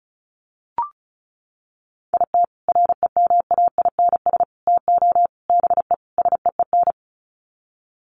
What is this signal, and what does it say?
Morse code, heard as a keyed tone, about 34 wpm, 700 Hz. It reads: IT REMAINS TO BE SEEN
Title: Sets of 5 Words Encore 35wpm